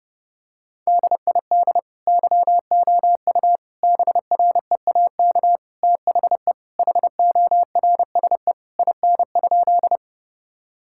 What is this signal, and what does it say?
Morse code, heard as a keyed tone, about 32 wpm, 700 Hz. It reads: DID YOU BREAK THE HORSE IN?
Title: Sentences from Top 1000 Words - Review Part 6 - 30wpm